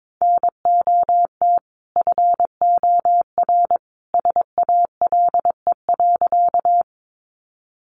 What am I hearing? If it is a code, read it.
Morse code, heard as a keyed tone, about 22 wpm, 700 Hz. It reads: NOT FOR SALE.
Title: Sentences from Top 1400 Words 22wpm